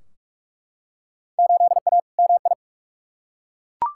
Morse code, heard as a keyed tone, about 41 wpm, 700 Hz. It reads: 8A MI
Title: Field Day Exchanges Encore - 45wpm